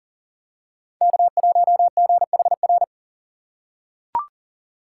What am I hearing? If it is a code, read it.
Morse code, heard as a keyed tone, about 39 wpm, 700 Hz. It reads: K1GHR
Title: USA Call Signs - 5 Characters - 3x Character Spacing - 40wpm